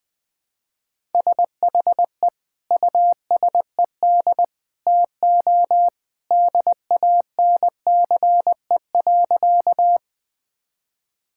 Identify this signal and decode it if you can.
Morse code, heard as a keyed tone, about 20 wpm, 700 Hz. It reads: SHE USED TO DANCE.
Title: Sentences with 2-Word Phrases - Spoken Beginning - 20wpm